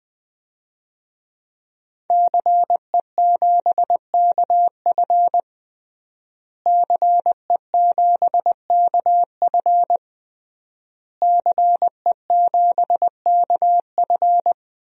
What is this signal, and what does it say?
Morse code, heard as a keyed tone, about 20 wpm, 700 Hz. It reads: CE7KF CE7KF CE7KF
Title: DXCC-100 Call Signs - 5 Characters - Repeated 3x - 20wpm